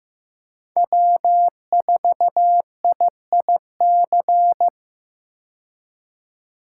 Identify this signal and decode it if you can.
Morse code, heard as a keyed tone, about 15 wpm, 700 Hz. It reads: W4IIC
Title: K1USN SST QSO Practice - 15wpm